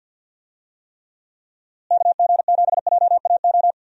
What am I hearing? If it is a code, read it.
Morse code, heard as a keyed tone, about 46 wpm, 700 Hz. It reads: KG7JAO
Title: USA Call Signs - 6 Characters - Repeated 3x - 50wpm